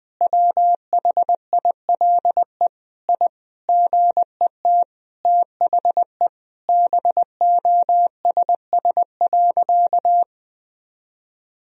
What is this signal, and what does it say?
Morse code, heard as a keyed tone, about 20 wpm, 700 Hz. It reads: WHILE I GET THE BOSS.
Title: Sentences from Top 1000 Words - Review Part 4 - 20wpm